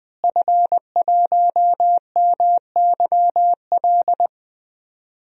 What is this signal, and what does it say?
Morse code, heard as a keyed tone, about 21 wpm, 700 Hz. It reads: F1MYL
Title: DXCC-100 Call Signs - 5 Characters - Repeated 3x - 20wpm